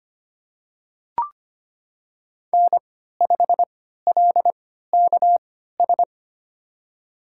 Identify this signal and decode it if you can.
Morse code, heard as a keyed tone, about 25 wpm, spaced as for 14 wpm, 700 Hz. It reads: N5LKS
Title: USA Call Signs - 5 Characters - 3x Character Spacing - 25wpm